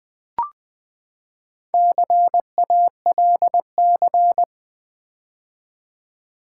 Morse code, heard as a keyed tone, about 20 wpm, 700 Hz. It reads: CALC
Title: Build A Word - 4 Letters At A Time - 11-Letter Words - 20wpm